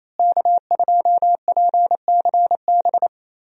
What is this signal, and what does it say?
Morse code, heard as a keyed tone, about 28 wpm, 700 Hz. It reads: K2PCB
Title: USA Call Signs - 5 Characters - Repeated 2x - 28wpm